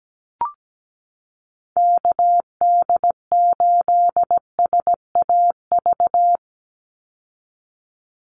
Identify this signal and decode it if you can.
Morse code, heard as a keyed tone, about 17 wpm, 700 Hz. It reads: KD8SAV
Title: Call Signs Repeated 2x - 17wpm